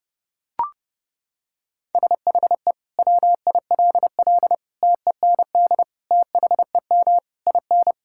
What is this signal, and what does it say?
Morse code, heard as a keyed tone, about 30 wpm, 700 Hz. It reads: SHE WILL TEND THEM IN
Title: Sentences from Top 1000 Words - Review Part 5 - 30wpm